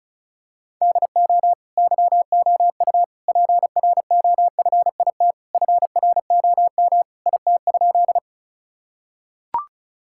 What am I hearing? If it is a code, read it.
Morse code, heard as a keyed tone, about 36 wpm, 700 Hz. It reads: DO YOU PROFIT FROM IT?
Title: Sentences from Top 1000 Words - Review Part 5 - 35wpm